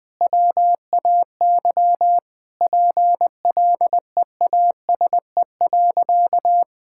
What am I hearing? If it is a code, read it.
Morse code, heard as a keyed tone, about 21 wpm, 700 Hz. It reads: WAY PLEASE.
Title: Sentences with 4-Word Phrases - 20wpm